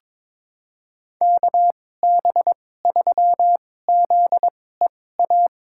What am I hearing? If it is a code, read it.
Morse code, heard as a keyed tone, about 21 wpm, spaced as for 16 wpm, 700 Hz. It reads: KB3ZEA
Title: USA Call Signs - 2x Character Spacing - 22wpm